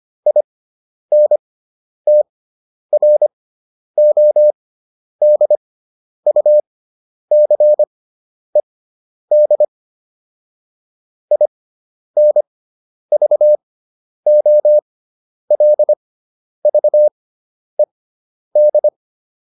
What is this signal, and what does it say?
Morse code, heard as a keyed tone, about 25 wpm, spaced as for 10 wpm, 600 Hz. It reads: INTRODUCED INVOLVED